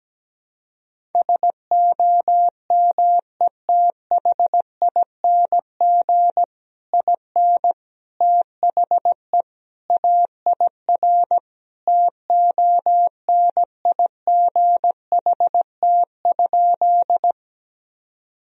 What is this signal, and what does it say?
Morse code, heard as a keyed tone, about 17 wpm, 700 Hz. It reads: SOMETHING IN THE AIR TONIGHT?
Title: Sentences from Top 1000 Words - Review Part 1 - 17wpm